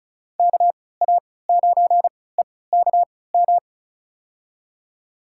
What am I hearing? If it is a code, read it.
Morse code, heard as a keyed tone, about 34 wpm, spaced as for 20 wpm, 700 Hz. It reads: KA9EKM
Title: USA Call Signs - 3x Character Spacing - 35wpm